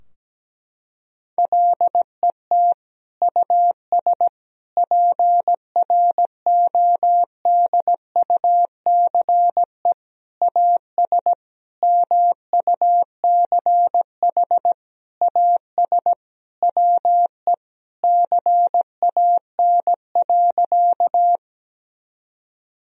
Morse code, heard as a keyed tone, about 17 wpm, 700 Hz. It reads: LET US PRODUCE AS MUCH AS WE CAN.
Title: Sentences from Top 1000 Words - Review Part 3 - 17wpm